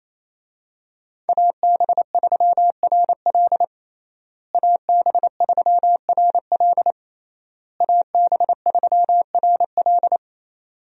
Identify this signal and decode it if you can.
Morse code, heard as a keyed tone, about 28 wpm, 700 Hz. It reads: AB3RL AB3RL AB3RL
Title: USA Call Signs - 5 Characters - Repeated 3x - 28wpm